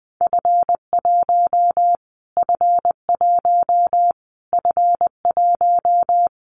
Morse code, heard as a keyed tone, about 20 wpm, 700 Hz. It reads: F1 F1 F1
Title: Callsign Prefixes for Top 25 DXCC - Repeated 3x - 20wpm